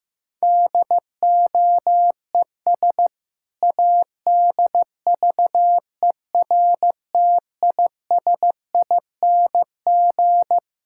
Morse code, heard as a keyed tone, about 15 wpm, 700 Hz. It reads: DOES ADVERTISING
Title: Sentences from Top 1000 Words - Review Part 6 - 15wpm